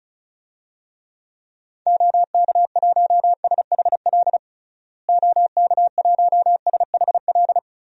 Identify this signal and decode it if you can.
Morse code, heard as a keyed tone, about 34 wpm, 700 Hz. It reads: OK1SHL OK1SHL
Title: DXCC-100 Call Signs - 6 Characters - Repeated 3x - 35wpm